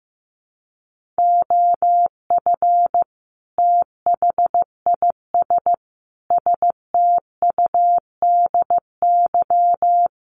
Morse code, heard as a keyed tone, about 15 wpm, 700 Hz. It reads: OF THIS STUDY
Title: Sets of 5 Words 15wpm